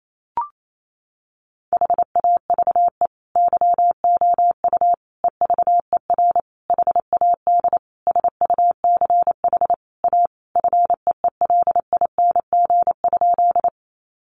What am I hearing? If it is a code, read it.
Morse code, heard as a keyed tone, about 28 wpm, 700 Hz. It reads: HAVE YOU EVER HAD SUCH A FEELING?
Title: Sentences from Top 1200 Words 28wpm